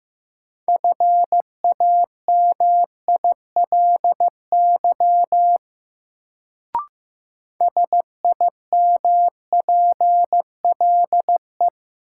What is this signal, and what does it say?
Morse code, heard as a keyed tone, about 15 wpm, 700 Hz. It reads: FAMILY SIMPLE
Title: Sentences from Top 400 Words - New Word Drill - 15wpm